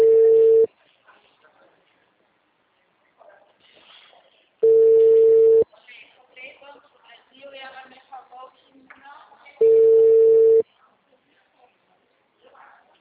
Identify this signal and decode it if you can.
Ringback tone (China, 450 Hz), with speech.